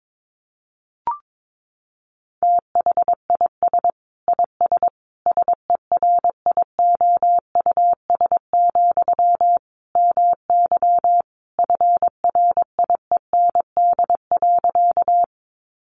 Morse code, heard as a keyed tone, about 22 wpm, 700 Hz. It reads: THIS IS SERIOUS, MY FRIEND.
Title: Sentences from Top 1100 Words 22wpm